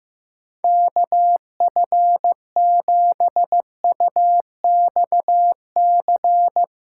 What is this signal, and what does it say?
Morse code, heard as a keyed tone, about 15 wpm, 700 Hz. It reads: KF7UXC
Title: K1USN SST QSO Practice - 15wpm